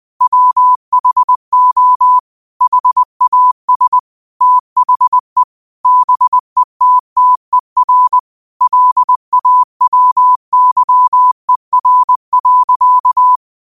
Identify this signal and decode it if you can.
Morse code, heard as a keyed tone, about 20 wpm, 1000 Hz. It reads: WHO HAS THE BETTER LAWYER.
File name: qotdfiles / 20wpm